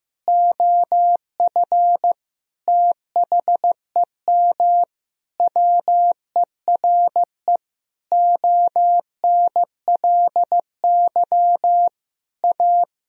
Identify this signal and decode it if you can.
Morse code, heard as a keyed tone, about 15 wpm, 700 Hz. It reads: OF THEM WERE ONLY A
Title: Sentences from Top 1000 Words - Review Part 2 - 15wpm